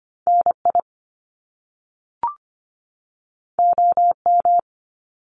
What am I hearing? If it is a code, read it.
Morse code, heard as a keyed tone, about 25 wpm, 700 Hz. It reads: NI OM
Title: Top Two Letter Combinations in Words 25wpm